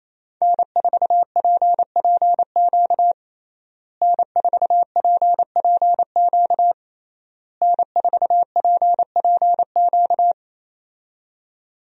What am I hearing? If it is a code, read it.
Morse code, heard as a keyed tone, about 27 wpm, 700 Hz. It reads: N4PPQ N4PPQ N4PPQ
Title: USA Call Signs - 5 Characters - Repeated 3x - 28wpm